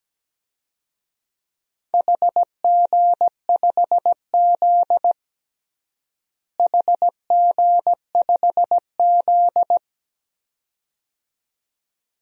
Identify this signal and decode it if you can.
Morse code, heard as a keyed tone, about 17 wpm, 700 Hz. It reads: HG5Z HG5Z
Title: DXCC-100 Call Signs - 4 Characters - Repeated 2x - 17wpm